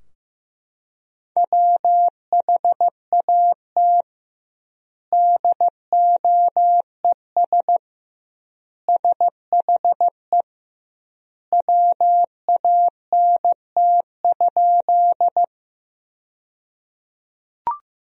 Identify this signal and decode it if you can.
Morse code, heard as a keyed tone, about 16 wpm, 700 Hz. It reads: WHAT DOES SHE WANT?
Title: 2x Word-Spacing - Sentences from Top 200 Words - 15wpm